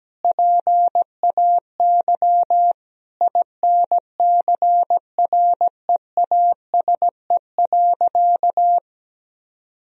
Morse code, heard as a keyed tone, about 17 wpm, 700 Hz. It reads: PAY INCREASE.
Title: Sentences from Top 1000 Words - Review Part 4 - 17wpm